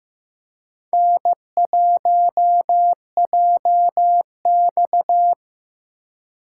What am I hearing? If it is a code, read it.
Morse code, heard as a keyed tone, about 15 wpm, 700 Hz. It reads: N1JX
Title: USA Call Signs - 4 Characters - 4x Character Spacing - 15wpm